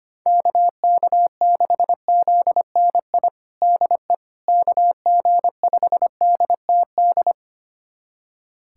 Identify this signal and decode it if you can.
Morse code, heard as a keyed tone, about 25 wpm, 700 Hz. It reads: KK6ZNI DE KG5DTD
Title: USA Callsign DE Callsign - 6 Characters - 25wpm